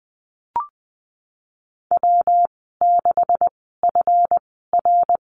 Morse code, heard as a keyed tone, about 20 wpm, spaced as for 15 wpm, 700 Hz. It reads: W6FR
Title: 140 Lesson 23g - Letter f - Call Signs 20wpm